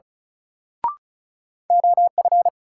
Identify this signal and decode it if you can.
Morse code, heard as a keyed tone, about 34 wpm, 700 Hz. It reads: OF
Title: Sentences from Top 100 Words - New Word Drill - 35wpm